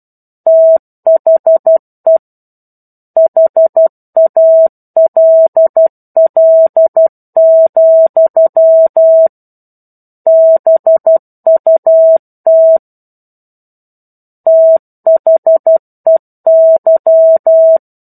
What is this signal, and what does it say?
Morse code, heard as a keyed tone, about 12 wpm, 650 Hz. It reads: THE HALL, BUT THEY